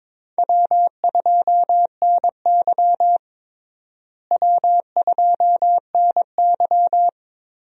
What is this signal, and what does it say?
Morse code, heard as a keyed tone, about 22 wpm, 700 Hz. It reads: W2NY W2NY
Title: USA Call Signs - 4 Characters - Repeated 3x - 22wpm